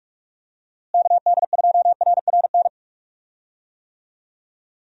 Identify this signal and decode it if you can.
Morse code, heard as a keyed tone, about 46 wpm, 700 Hz. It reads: KD2RRN
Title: USA Call Signs - 6 Characters - Repeated 3x - 45wpm